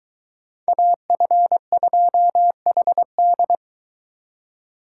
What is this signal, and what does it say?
Morse code, heard as a keyed tone, about 22 wpm, 700 Hz. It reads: AF2HD